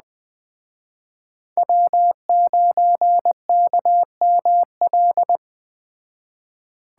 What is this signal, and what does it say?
Morse code, heard as a keyed tone, about 20 wpm, 700 Hz. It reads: W9KML
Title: Speed Racing - USA 1x3 Call Signs 17wpm